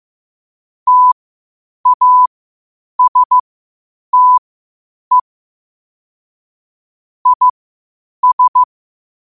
Morse code, heard as a keyed tone, about 15 wpm, spaced as for 8 wpm, 1000 Hz. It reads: TASTE IS
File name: qotdfiles / 8wpm